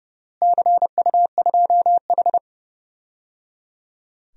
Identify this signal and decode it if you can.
Morse code, heard as a keyed tone, about 30 wpm, 700 Hz. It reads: CU2H
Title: DXCC-100 Call Signs - 4 Characters - Repeated 3x - 30wpm